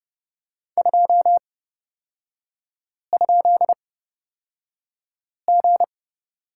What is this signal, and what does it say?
Morse code, heard as a keyed tone, about 30 wpm, spaced as for 5 wpm, 700 Hz. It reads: 2?G
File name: Lesson 37e - Letter x - Three Characters 30wpm